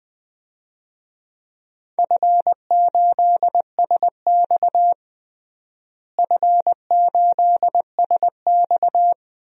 Morse code, heard as a keyed tone, about 20 wpm, 700 Hz. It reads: F8SX F8SX
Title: DXCC-100 Call Signs - 4 Characters - Repeated 2x - 20wpm